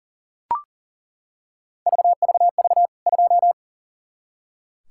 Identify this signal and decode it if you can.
Morse code, heard as a keyed tone, about 37 wpm, 700 Hz. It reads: VVV 2
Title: K1USN SST QSO Practice 40wpm